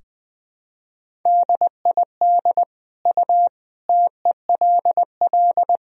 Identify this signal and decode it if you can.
Morse code, heard as a keyed tone, about 21 wpm, 700 Hz. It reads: DID U TELL
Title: Sentences from Top 400 Words and Abbreviations 20wpm